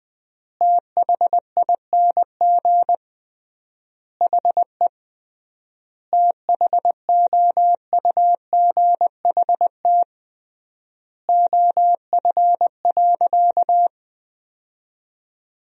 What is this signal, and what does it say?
Morse code, heard as a keyed tone, about 20 wpm, 700 Hz. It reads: THING HE THOUGHT OF.
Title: 3x Word-Spacing - Sentences from Top 300 Words - 20wpm